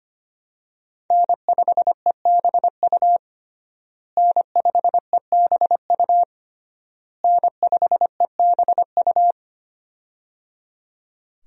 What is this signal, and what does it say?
Morse code, heard as a keyed tone, about 25 wpm, 700 Hz. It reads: N5EBU N5EBU N5EBU
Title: USA Call Signs - 5 Characters - Repeated 3x - 25wpm